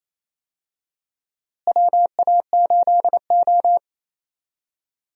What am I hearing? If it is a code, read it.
Morse code, heard as a keyed tone, about 28 wpm, 700 Hz. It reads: WA8O
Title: USA Call Signs - 4 Characters - Repeated 3x - 28wpm